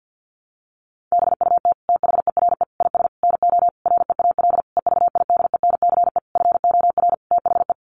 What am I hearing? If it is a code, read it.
Morse code, heard as a keyed tone, about 50 wpm, 700 Hz. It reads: BUT THERE IS NO REAL EVIDENCE FOR THE
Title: Sentences from Top 1300 Words 50wpm